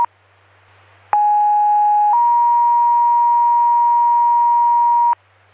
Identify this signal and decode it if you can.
Two-tone page, heard as a one-second tone, then a three-second tone, about 830 then 950 Hz. It is heard from 1.1 to 5.1 s.